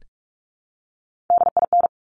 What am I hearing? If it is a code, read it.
Morse code, heard as a keyed tone, about 46 wpm, 700 Hz. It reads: DIN
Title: Top Three Letter Combinations in Words 45wpm